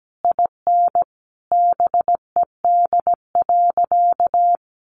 Morse code, heard as a keyed tone, about 17 wpm, 700 Hz. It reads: IN BED.
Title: Sentences from Top 1200 Words 17wpm